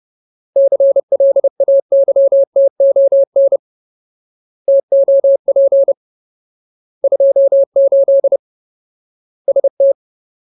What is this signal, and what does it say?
Morse code, heard as a keyed tone, about 30 wpm, 550 Hz. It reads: CLAYTON TOP 28 ST
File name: Ultimate Word List / 30WPM Ultimate